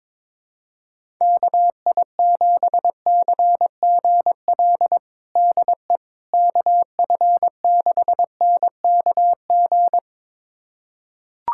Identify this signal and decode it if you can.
Morse code, heard as a keyed tone, about 21 wpm, 700 Hz. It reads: KI7CGL DE KF6NKG
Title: USA Callsign DE Callsign - 6 Characters - 22wpm